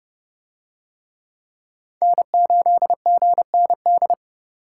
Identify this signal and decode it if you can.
Morse code, heard as a keyed tone, about 29 wpm, 700 Hz. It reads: N8GND